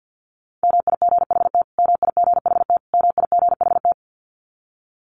Morse code, heard as a keyed tone, about 48 wpm, 700 Hz. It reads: MIGHT MIGHT MIGHT
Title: Top 300 Words Repeated 50wpm